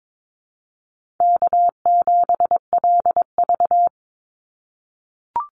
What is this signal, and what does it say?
Morse code, heard as a keyed tone, about 22 wpm, 700 Hz. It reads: K7LV
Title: Easy Call Signs - 22wpm